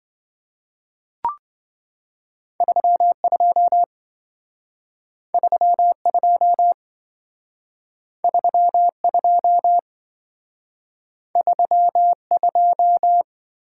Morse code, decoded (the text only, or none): 32 32 32 32